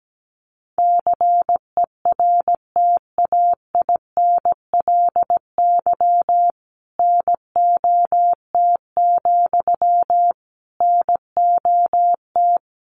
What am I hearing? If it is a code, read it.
Morse code, heard as a keyed tone, about 17 wpm, 700 Hz. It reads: CERTAINLY NOT, NOT
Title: Sentences from Top 900 Words 17wpm